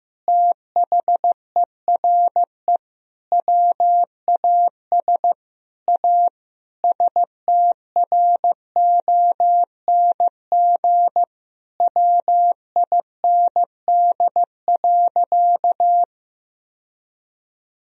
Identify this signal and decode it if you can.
Morse code, heard as a keyed tone, about 15 wpm, 700 Hz. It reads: THERE WAS A STRONG WIND.
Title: Sentences from Top 1000 Words - Review Part 4 - 15wpm